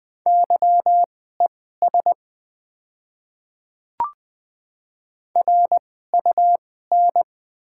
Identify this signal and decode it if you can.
Morse code, heard as a keyed tone, about 20 wpm, spaced as for 14 wpm, 690 Hz. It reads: YES RUN
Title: 2x Character Spacing - Top 500 Words - 20wpm